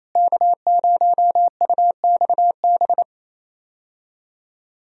Morse code, heard as a keyed tone, about 27 wpm, 700 Hz. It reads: K0UXB
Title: USA Call Signs - 5 Characters - Repeated 2x - 28wpm